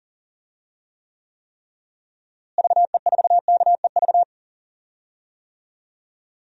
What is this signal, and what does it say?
Morse code, heard as a keyed tone, about 40 wpm, 700 Hz. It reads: VE4KEV
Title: DXCC-100 Call Signs - 6 Characters - Repeated 3x - 40wpm